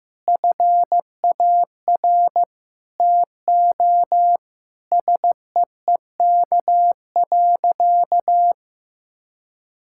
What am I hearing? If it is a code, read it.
Morse code, heard as a keyed tone, about 15 wpm, 700 Hz. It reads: FAR TO SEEK.
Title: Sentences from Top 1600 Words 15wpm